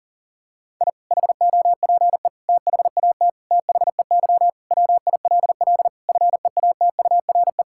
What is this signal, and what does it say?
Morse code, heard as a keyed tone, about 40 wpm, 700 Hz. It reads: I HOPE THAT THEY WILL FEATURE
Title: Sentences from Top 1000 Words - Review Part 5 - 40wpm